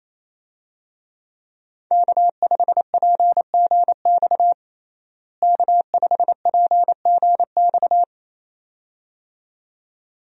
Morse code, heard as a keyed tone, about 28 wpm, 700 Hz. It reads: K5PGX K5PGX
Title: USA Call Signs - 5 Characters - Repeated 2x - 28wpm